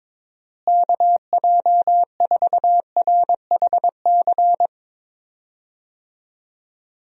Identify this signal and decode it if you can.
Morse code, heard as a keyed tone, about 22 wpm, 700 Hz. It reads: KJ4RHC
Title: USA Call Signs - 6 Characters - Repeated 2x - 22wpm